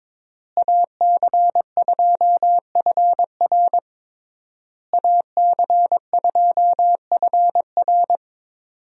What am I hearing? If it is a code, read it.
Morse code, heard as a keyed tone, about 22 wpm, 700 Hz. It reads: AC2FR AC2FR
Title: USA Call Signs - 5 Characters - Repeated 3x - 22wpm